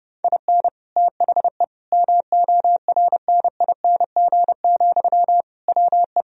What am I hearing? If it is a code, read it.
Morse code, heard as a keyed tone, about 32 wpm, 700 Hz. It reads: IN THE MORNING, WE
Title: Sentences from Top 1000 Words - Review Part 3 - 30wpm